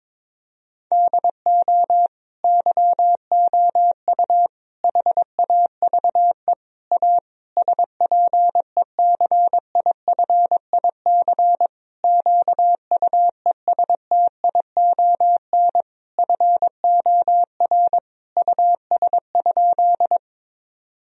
Morse code, heard as a keyed tone, about 22 wpm, 700 Hz. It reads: DO YOU HAVE A SPECIFIC QUESTION FOR US?
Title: Sentences from Top 1000 Words - Review Part 3 - 22wpm